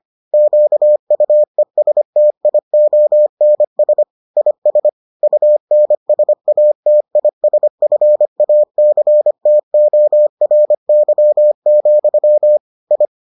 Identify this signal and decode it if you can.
Morse code, heard as a keyed tone, about 25 wpm, 600 Hz. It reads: QUESTIONS IS UNSATISFACTORY, I